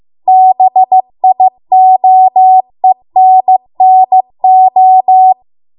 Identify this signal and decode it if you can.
Morse code, heard as a keyed tone, about 15 wpm, 750 Hz. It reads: BIOENNO